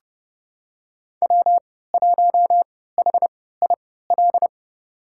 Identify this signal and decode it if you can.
Morse code, heard as a keyed tone, about 32 wpm, spaced as for 17 wpm, 700 Hz. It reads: W1HIL
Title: USA Call Signs - 3x Character Spacing - 30wpm